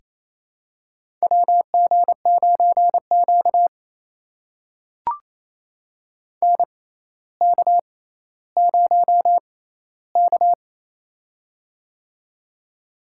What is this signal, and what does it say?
Morse code, heard as a keyed tone, about 28 wpm, spaced as for 14 wpm, 700 Hz. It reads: WG9Q NK0K
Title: USA Call Signs - 4 Characters - 6x Character Spacing - 28wpm